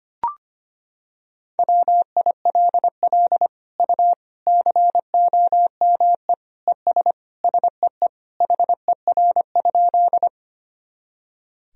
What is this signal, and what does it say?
Morse code, heard as a keyed tone, about 25 wpm, 700 Hz. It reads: WILL U COME ES SEE HER?
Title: Sentences from Top 100 Words and Abbreviations 25wpm